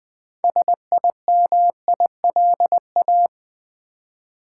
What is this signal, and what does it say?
Morse code, heard as a keyed tone, about 20 wpm, 700 Hz. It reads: SIMILA
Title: Build A Word - 3 Letters At A Time - 9-Letter Words - 20wpm